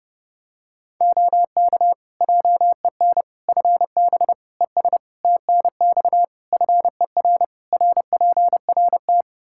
Morse code, heard as a keyed tone, about 30 wpm, 700 Hz. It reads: OK JEN FB ES TNX FER RPRT
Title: Practice QSO - Exchange 2 - Info - 30wpm